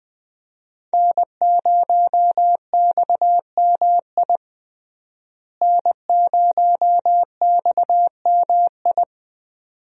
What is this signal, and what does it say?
Morse code, heard as a keyed tone, about 20 wpm, 700 Hz. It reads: N0XMI N0XMI
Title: USA Call Signs - 5 Characters - Repeated 2x - 20wpm